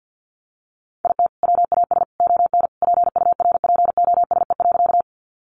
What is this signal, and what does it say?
Morse code, heard as a keyed tone, about 50 wpm, 700 Hz. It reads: IT WAS ON PURPOSE.
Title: Sentences from Top 800 Words 50wpm